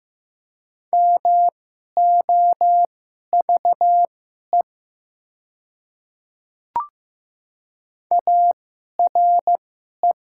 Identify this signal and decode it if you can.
Morse code, heard as a keyed tone, about 15 wpm, spaced as for 11 wpm, 700 Hz. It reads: MOVE ARE
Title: 2x Character Spacing - Top 200 Words - 15wpm